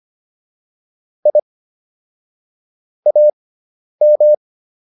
Morse code, heard as a keyed tone, about 25 wpm, spaced as for 10 wpm, 600 Hz. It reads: I AM
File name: QSO409 10